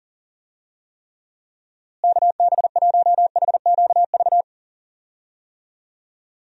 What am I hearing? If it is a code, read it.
Morse code, heard as a keyed tone, about 37 wpm, 700 Hz. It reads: KB1HQV
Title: USA Call Signs - 6 Characters - Repeated 2x - 40wpm